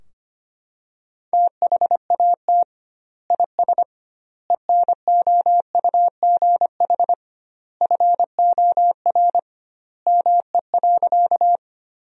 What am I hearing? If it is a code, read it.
Morse code, heard as a keyed tone, about 25 wpm, 700 Hz. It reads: THAT IS ENOUGH FOR ME.
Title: 2x Word-Spacing - Sentences from Top 400 Words - 25wpm